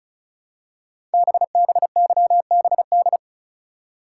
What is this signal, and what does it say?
Morse code, heard as a keyed tone, about 36 wpm, 700 Hz. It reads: BBYBD
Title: Code Groups - 5 Letters - 35wpm